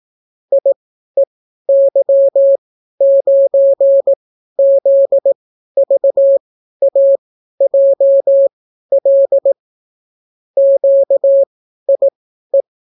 Morse code, heard as a keyed tone, about 18 wpm, spaced as for 12 wpm, 550 Hz. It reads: IEY9ZVAJL QIE